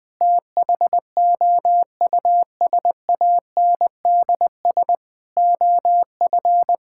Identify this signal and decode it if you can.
Morse code, heard as a keyed tone, about 21 wpm, 700 Hz. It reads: THOUSANDS OF
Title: Sentences with 4-Word Phrases - Spoken Beginning - 20wpm